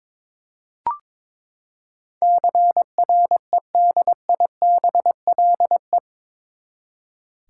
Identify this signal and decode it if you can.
Morse code, heard as a keyed tone, about 22 wpm, 700 Hz. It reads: CREDIBLE
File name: practice / 22wpm / Eight Letter Words - 22wpm